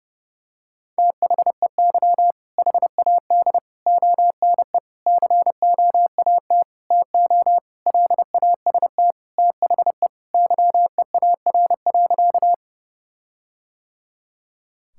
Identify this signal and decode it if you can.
Morse code, heard as a keyed tone, about 30 wpm, 700 Hz. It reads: THEY HAD ONE COAT TO LAST THE YEAR.
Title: Sentences from Top 1600 Words 30wpm